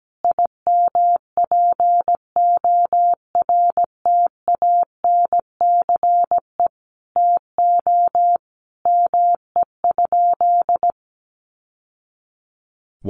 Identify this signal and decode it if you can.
Morse code, heard as a keyed tone, about 17 wpm, 700 Hz. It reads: IMPORTANCE TO ME?
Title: Sentences from Top 1200 Words 17wpm